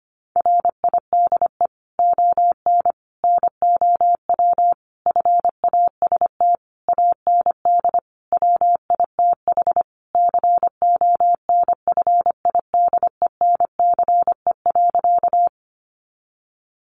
Morse code, heard as a keyed tone, about 25 wpm, 700 Hz. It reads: RIDE ON NOW FAST AND WITH CONFIDENCE.
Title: Sentences from Top 1500 Words 25wpm